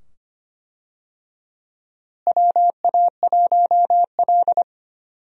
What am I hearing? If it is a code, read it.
Morse code, heard as a keyed tone, about 25 wpm, 700 Hz. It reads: WA1L
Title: USA Call Signs - 4 Characters - Repeated 2x - 25wpm